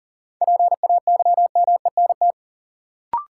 Morse code, heard as a keyed tone, about 39 wpm, 700 Hz. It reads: PAYMENT
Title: Sentences from Top 1200 Words - New Word Drill Alt - 40wpm